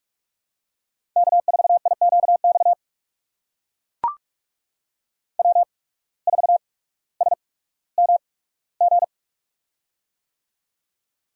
Morse code, heard as a keyed tone, about 44 wpm, spaced as for 19 wpm, 700 Hz. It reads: K4IQX W4SMG
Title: USA Call Signs - 5 Characters - 8x Character Spacing - 45wpm